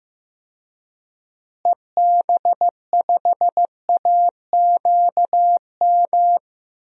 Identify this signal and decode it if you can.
Morse code, heard as a keyed tone, about 14 wpm, 700 Hz. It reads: EB5AQM
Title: DXCC-100 Call Signs - 6 Characters - Repeated 3x - 15wpm